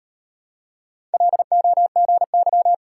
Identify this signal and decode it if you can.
Morse code, heard as a keyed tone, about 35 wpm, 700 Hz. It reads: LOGY